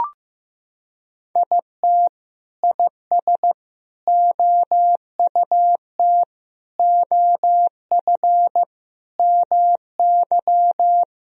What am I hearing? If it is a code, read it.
Morse code, heard as a keyed tone, about 15 wpm, 700 Hz. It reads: IT IS OUT OF MY
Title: Sentences from Top 1000 Words - Review Part 2 - 15wpm